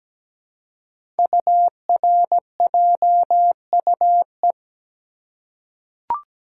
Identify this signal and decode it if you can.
Morse code, heard as a keyed tone, about 17 wpm, 700 Hz. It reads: URJUE
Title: Code Groups - 5 Letters - 17wpm